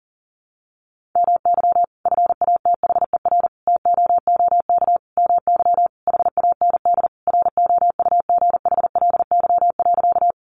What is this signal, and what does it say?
Morse code, heard as a keyed tone, about 40 wpm, 700 Hz. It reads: MY FATHER TOOK MY HAND ROUGHLY.
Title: Sentences from Top 1500 Words 40wpm